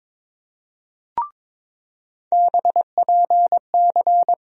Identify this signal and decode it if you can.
Morse code, heard as a keyed tone, about 22 wpm, 700 Hz. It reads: BPC
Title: USA Call Sign Postfixes - 3 Letters 22wpm